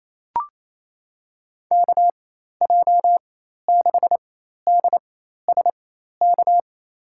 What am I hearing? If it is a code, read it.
Morse code, heard as a keyed tone, about 27 wpm, spaced as for 13 wpm, 700 Hz. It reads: KJ6DSK
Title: USA Call Signs - 6 Characters - 4x Character Spacing - 28wpm